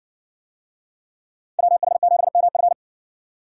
Morse code, heard as a keyed tone, about 60 wpm, 700 Hz. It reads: WH7MF